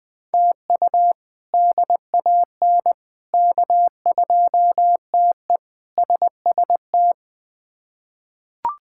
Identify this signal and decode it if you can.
Morse code, heard as a keyed tone, about 21 wpm, 700 Hz. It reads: TU DAN K2TE SST
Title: K1USN SST QSO Practice - 20wpm